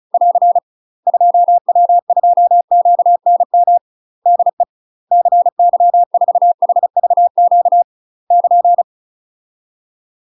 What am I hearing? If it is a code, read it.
Morse code, heard as a keyed tone, about 36 wpm, 700 Hz. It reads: <AR> 2W2QNM DE CY4HVQ <KN>
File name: UK MASTER QSOs TRAINING / 35WPM / UK MASTER QSOs 0009